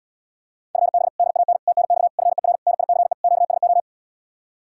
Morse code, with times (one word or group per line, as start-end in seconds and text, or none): 0.8-1.1 FB
1.2-1.6 DAN
1.7-2.1 TNX
2.2-2.6 FER
2.7-3.1 NICE
3.2-3.8 QSO